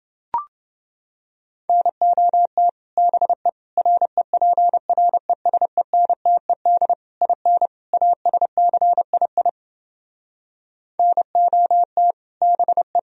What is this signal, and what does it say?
Morse code, heard as a keyed tone, about 29 wpm, 700 Hz. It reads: NOT BE REPRESENTED IN ASCII NOT BE